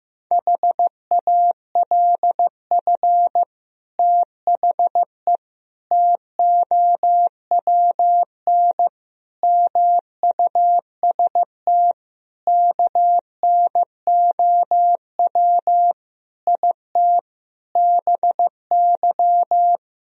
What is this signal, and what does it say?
Morse code, heard as a keyed tone, about 15 wpm, 700 Hz. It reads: HALF THE TOWN MUST KNOW IT BY
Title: Sentences from Top 1000 Words - Review Part 2 - 15wpm